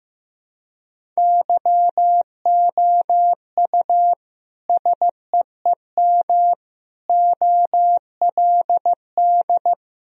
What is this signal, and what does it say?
Morse code, heard as a keyed tone, about 15 wpm, 700 Hz. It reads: YOU SEEM OLD
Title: Sentences from Top 1000 Words - Review Part 1 - 15wpm